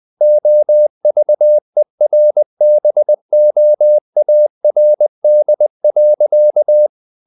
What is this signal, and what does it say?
Morse code, heard as a keyed tone, about 20 wpm, 600 Hz. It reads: OVERBOARD.
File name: the old man and the sea / the old man and the sea part 1 20wpm